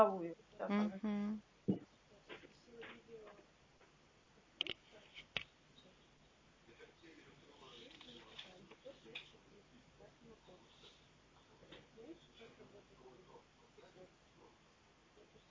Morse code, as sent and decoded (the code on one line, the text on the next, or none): none